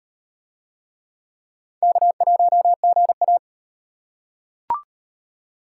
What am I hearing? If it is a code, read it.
Morse code, heard as a keyed tone, about 38 wpm, 700 Hz. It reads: K1GA